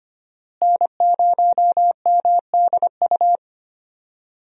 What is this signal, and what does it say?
Morse code, heard as a keyed tone, about 25 wpm, 700 Hz. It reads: N0MDU
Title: USA Call Signs - 5 Characters - Repeated 3x - 25wpm